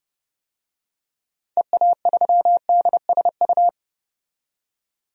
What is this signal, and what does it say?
Morse code, heard as a keyed tone, about 30 wpm, 700 Hz. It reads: EA3DSU
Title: DXCC-100 Call Signs - 6 Characters - 30wpm